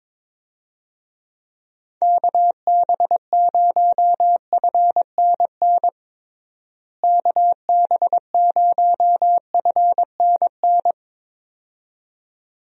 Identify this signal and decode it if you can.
Morse code, heard as a keyed tone, about 22 wpm, 700 Hz. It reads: KB0FNN KB0FNN
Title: USA Call Signs - 6 Characters - Repeated 2x - 22wpm